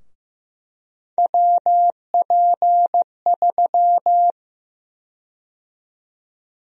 Morse code, heard as a keyed tone, about 15 wpm, 700 Hz. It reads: WP3
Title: USA Call Sign Prefixes - 15wpm